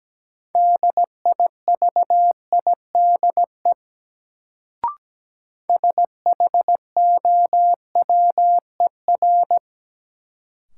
Morse code, heard as a keyed tone, about 17 wpm, 700 Hz. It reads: DIVIDE SHOWER
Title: Sentences from Top 1600 Words - New Word Drill - 17wpm